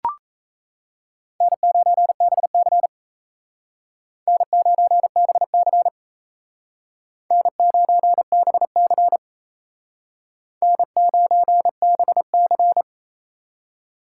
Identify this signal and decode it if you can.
Morse code, heard as a keyed tone, about 35 wpm, 700 Hz. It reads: N9BC N9BC N9BC N9BC
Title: Speed Racing - USA Call Signs 28wpm